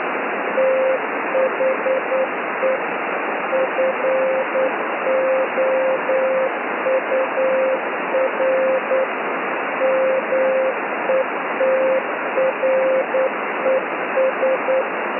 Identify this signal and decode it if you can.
Morse code, heard as a keyed tone, about 9 wpm, 530 Hz, with radio noise. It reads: THE FOUR METRES
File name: s55zmb beacon